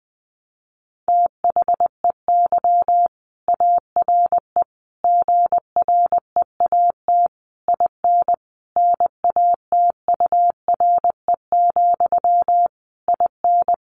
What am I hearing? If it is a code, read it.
Morse code, heard as a keyed tone, about 20 wpm, 700 Hz. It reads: THEY ARE GREAT IN NATURE, IN
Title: Sentences from Top 1400 Words 20wpm